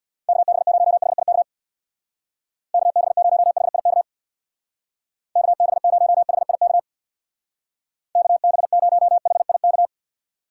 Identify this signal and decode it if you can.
Morse code, heard as a keyed tone, about 65 wpm, 700 Hz. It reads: KB0HIK KB0HIK KB0HIK KB0HIK